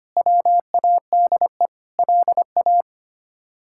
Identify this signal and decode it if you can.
Morse code, heard as a keyed tone, about 25 wpm, 700 Hz. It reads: WADE LA
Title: K1USN SST QSO Practice - 25wpm